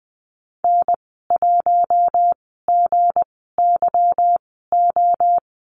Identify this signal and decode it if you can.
Morse code, heard as a keyed tone, about 20 wpm, spaced as for 14 wpm, 700 Hz. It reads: N1GYO
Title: 238 Lesson 38g - Letter q - Call Signs 20wpm